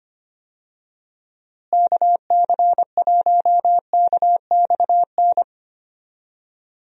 Morse code, heard as a keyed tone, about 25 wpm, 700 Hz. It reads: KC1KXN
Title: USA Call Signs - 6 Characters - Repeated 2x - 25wpm